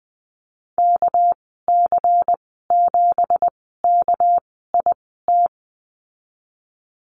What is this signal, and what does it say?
Morse code, heard as a keyed tone, about 20 wpm, spaced as for 14 wpm, 700 Hz. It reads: KC7KIT
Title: 207 Lesson 33g - Letter j - Call Signs 20wpm